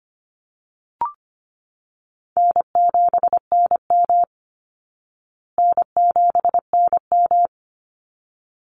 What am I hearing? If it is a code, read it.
Morse code, heard as a keyed tone, about 25 wpm, 700 Hz. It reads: N7NM N7NM
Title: Call Signs Repeated 3x - 25wpm